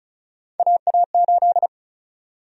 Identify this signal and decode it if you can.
Morse code, heard as a keyed tone, about 36 wpm, 700 Hz. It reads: AA8
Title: USA Call Sign Prefixes - Repeated 2x - 35wpm